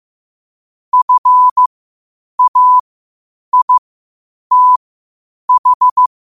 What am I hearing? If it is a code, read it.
Morse code, heard as a keyed tone, about 15 wpm, spaced as for 8 wpm, 1000 Hz. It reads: FAITH
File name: qotdfiles / 8wpm